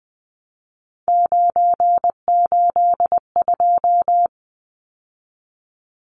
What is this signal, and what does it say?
Morse code, heard as a keyed tone, about 20 wpm, 700 Hz. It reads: 982
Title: Numbers - 3-Digit 20wpm